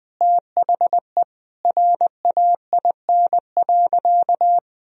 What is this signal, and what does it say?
Morse code, heard as a keyed tone, about 21 wpm, 700 Hz. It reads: THE RAIN.
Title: Sentences from Top 1000 Words - Review Part 1 - 20wpm